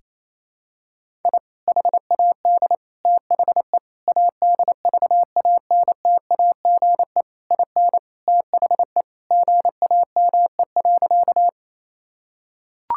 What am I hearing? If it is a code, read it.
Morse code, heard as a keyed tone, about 28 wpm, 700 Hz. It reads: I HAD THE ADVANTAGE IN THE GAME.
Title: Sentences from Top 1000 Words - Review Part 6 - 28wpm